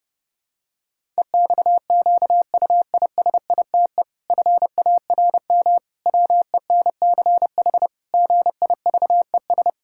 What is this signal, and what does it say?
Morse code, heard as a keyed tone, about 32 wpm, 700 Hz. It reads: EXQUISITE FARM WENCH GIVES
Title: Pangram Challenge v2 - 30wpm